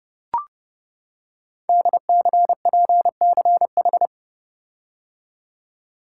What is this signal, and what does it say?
Morse code, heard as a keyed tone, about 30 wpm, 700 Hz. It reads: DCPCH
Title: Code Groups - 5 Letters - 30wpm